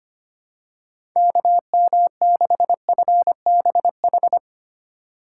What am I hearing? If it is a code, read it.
Morse code, heard as a keyed tone, about 25 wpm, 700 Hz. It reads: KM6FBH
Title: USA Call Signs - 6 Characters - 4x Character Spacing - 25wpm